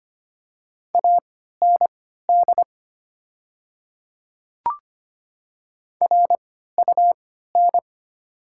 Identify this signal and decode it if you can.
Morse code, heard as a keyed tone, about 25 wpm, spaced as for 14 wpm, 700 Hz. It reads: AND RUN
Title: 3x Character Spacing - Top 500 Words - 25wpm